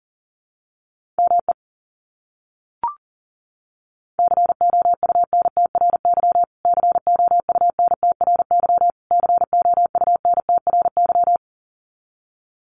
Morse code, heard as a keyed tone, about 40 wpm, 700 Hz. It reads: ME COUNTRY COUNTRY COUNTRY
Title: Top 300 Words Repeated 40wpm